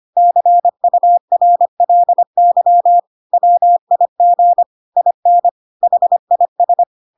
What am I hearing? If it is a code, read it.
Morse code, heard as a keyed tone, about 25 wpm, 700 Hz. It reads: CURLY WIG IN HIS
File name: Pinocchio / Pinocchio ch2 25 25 ik1hge